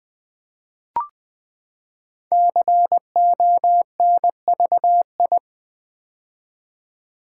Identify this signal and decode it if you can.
Morse code, heard as a keyed tone, about 20 wpm, 700 Hz. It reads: CONVI
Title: Build A Word - 5 Letters At A Time - 10-Letter Words - 20wpm